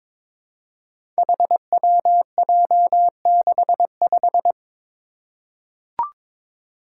Morse code, heard as a keyed tone, about 22 wpm, 700 Hz. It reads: HWJ65
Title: Five Challenging Letter-Number Combinations 22wpm